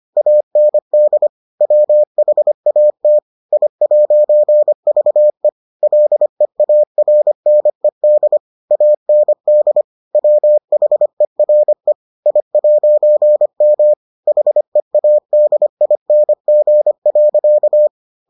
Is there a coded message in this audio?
Yes: Morse code from 0.2 to 17.9 s, heard as a keyed tone, about 25 wpm, 600 Hz. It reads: AND WHAT I'VE LEARNED AND WHERE I'M HEADING.